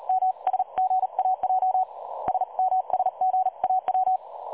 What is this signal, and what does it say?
Morse code, heard as a keyed tone, about 38 wpm, 750 Hz. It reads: M SGAJ SMSGAW
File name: dlazaawansowanych / nagraniazpasma / websdr recording 3565.0kHz